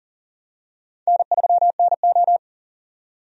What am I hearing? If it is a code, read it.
Morse code, heard as a keyed tone, about 40 wpm, 700 Hz. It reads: N3NO
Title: USA Call Signs - 4 Characters - 40wpm